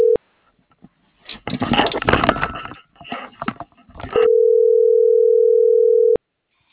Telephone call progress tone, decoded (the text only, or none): ringback tone